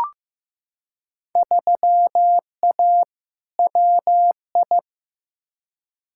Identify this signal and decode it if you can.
Morse code, heard as a keyed tone, about 15 wpm, 700 Hz. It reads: 3A WI
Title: Field Day Exchanges - 15wpm